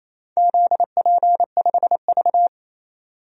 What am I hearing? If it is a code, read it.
Morse code, heard as a keyed tone, about 28 wpm, 700 Hz. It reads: ZP5V
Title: DXCC-100 Call Signs - 4 Characters - 28wpm